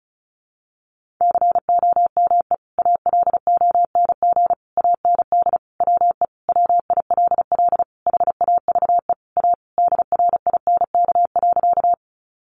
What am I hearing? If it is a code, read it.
Morse code, heard as a keyed tone, about 34 wpm, 700 Hz. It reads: COME ALONG AND WE WILL HAVE A DRINK.
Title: Sentences from Top 1500 Words 35wpm